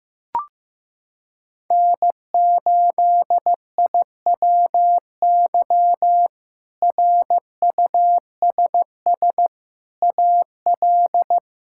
Morse code, heard as a keyed tone, about 15 wpm, 700 Hz. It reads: N8IWY RUSS AL
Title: K1USN SST QSO Practice - 15wpm